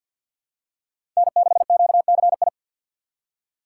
Morse code, heard as a keyed tone, about 51 wpm, 700 Hz. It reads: N6QCI
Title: USA Call Signs - 5 Characters - 3x Character Spacing - 50wpm